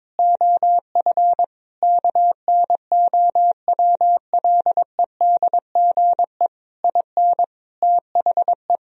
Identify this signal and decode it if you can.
Morse code, heard as a keyed tone, about 22 wpm, 700 Hz. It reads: OF KNOWLEDGE IN THE
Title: Sentences from Top 1000 Words - Review Part 4 - 22wpm